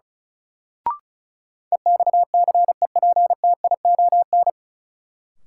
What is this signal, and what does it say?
Morse code, heard as a keyed tone, about 34 wpm, 700 Hz. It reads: EXCEPTION
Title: Sets of 4 Words - New Word Drill - 35wpm